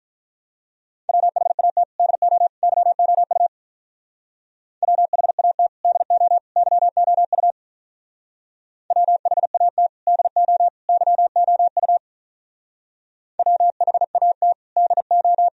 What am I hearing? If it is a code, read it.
Morse code, heard as a keyed tone, about 42 wpm, 700 Hz. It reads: WHAT DO YOU WHAT DO YOU WHAT DO YOU WHAT DO
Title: Speed-Racing - Sets of 3 Words 35wpm